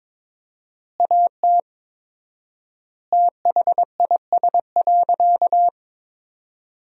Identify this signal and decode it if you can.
Morse code, heard as a keyed tone, about 22 wpm, 700 Hz. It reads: AT THIS.
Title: 4x Word-Spacing - Sentences from Top 300 Words - 22wpm